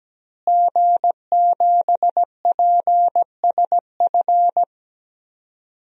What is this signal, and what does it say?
Morse code, heard as a keyed tone, about 17 wpm, 700 Hz. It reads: G7PSF
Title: DXCC-100 Call Signs - 5 Characters - Repeated 3x - 17wpm